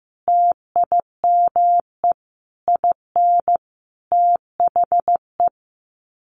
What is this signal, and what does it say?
Morse code, heard as a keyed tone, about 15 wpm, 700 Hz. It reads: TIME IN THE
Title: Sets of 5 Words Encore 15wpm